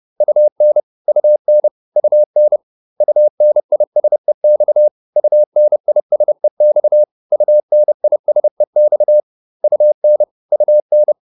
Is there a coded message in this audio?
Yes: Morse code from 0.2 to 11.1 s, heard as a keyed tone, about 30 wpm, 600 Hz. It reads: UN UN UN UNISEX UNISEX UNISEX UN UN